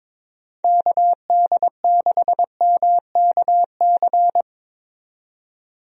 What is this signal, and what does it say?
Morse code, heard as a keyed tone, about 22 wpm, 700 Hz. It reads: KD6MKC
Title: USA Call Signs - 6 Characters - 22wpm